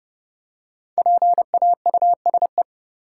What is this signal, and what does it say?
Morse code, heard as a keyed tone, about 30 wpm, 700 Hz. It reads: PAUSE
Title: Five Letter Words - 30wpm